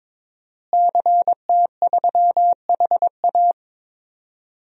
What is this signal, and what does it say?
Morse code, heard as a keyed tone, about 21 wpm, 700 Hz. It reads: CT3HA
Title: DXCC-100 Call Signs - 5 Characters - Repeated 2x - 22wpm